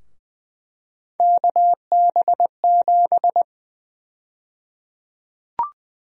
Morse code, heard as a keyed tone, about 20 wpm, 700 Hz. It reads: KB7
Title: USA Call Sign Prefixes - Repeated 2x - 20wpm